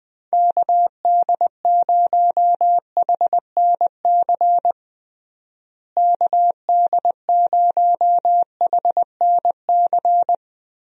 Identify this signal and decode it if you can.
Morse code, heard as a keyed tone, about 21 wpm, 700 Hz. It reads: KD0HNC KD0HNC
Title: USA Call Signs - 6 Characters - Repeated 2x - 20wpm